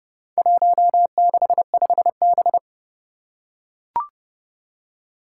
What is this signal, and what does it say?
Morse code, heard as a keyed tone, about 30 wpm, 700 Hz. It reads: 165B
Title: Four Challenging Letter-Number Combinations 30wpm